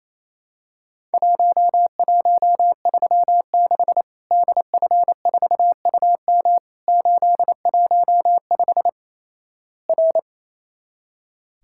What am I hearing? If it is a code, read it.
Morse code, heard as a keyed tone, about 28 wpm, 700 Hz. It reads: 1136 DF4UM 815 R
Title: Worked All Europe DX Contest - QTC Exchange - 28wpm